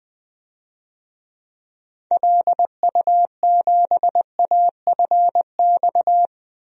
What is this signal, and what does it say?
Morse code, heard as a keyed tone, about 20 wpm, 700 Hz. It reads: LU7AFX
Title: DXCC-100 Call Signs - 6 Characters - 20wpm